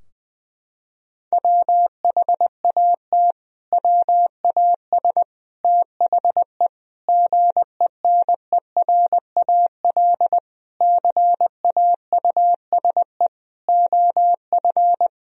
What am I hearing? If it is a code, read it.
Morse code, heard as a keyed tone, about 20 wpm, 700 Hz. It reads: WHAT WAS THE GENERAL CAUSE OF
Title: Sentences from Top 1000 Words - Review Part 3 - 20wpm